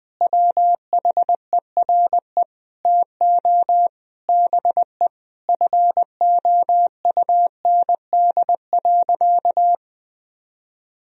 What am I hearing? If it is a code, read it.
Morse code, heard as a keyed tone, about 19 wpm, 700 Hz. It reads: WHERE TO BE FOUND.
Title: Sentences from Top 1000 Words - Review Part 3 - 20wpm